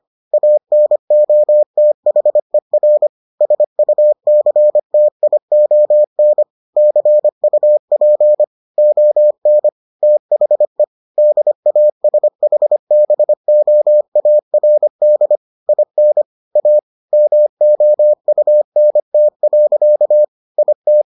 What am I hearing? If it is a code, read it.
Morse code, heard as a keyed tone, about 25 wpm, 600 Hz. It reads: ANOTHER SUCTION CUP ON THE DASHBOARD IN A MOUNT. IT